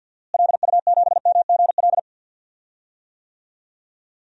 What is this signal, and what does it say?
Morse code, heard as a keyed tone, about 52 wpm, 700 Hz. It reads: LU7MGL
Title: DXCC-100 Call Signs - 6 Characters - Repeated 2x - 50wpm